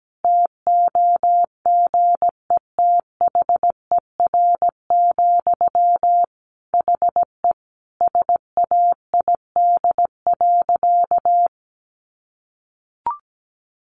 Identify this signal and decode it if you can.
Morse code, heard as a keyed tone, about 17 wpm, 700 Hz. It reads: TOGETHER, HE SAID.
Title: Sentences from Top 1100 Words 17wpm